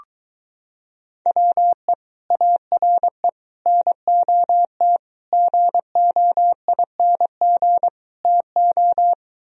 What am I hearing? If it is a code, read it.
Morse code, heard as a keyed tone, about 23 wpm, 700 Hz. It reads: WE ARE NOT GOING TO